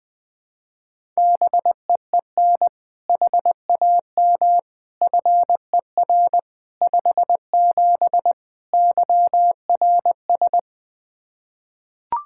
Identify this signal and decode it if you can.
Morse code, heard as a keyed tone, about 20 wpm, 700 Hz. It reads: BEEN HAM FER 57 YRS
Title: K1USN SST QSO Practice 20wpm